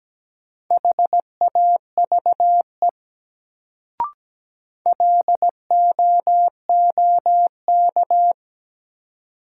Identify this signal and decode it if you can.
Morse code, heard as a keyed tone, about 17 wpm, 700 Hz. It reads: HAVE LOOK
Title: Sentences from Top 100 Words - New Word Drill - 17wpm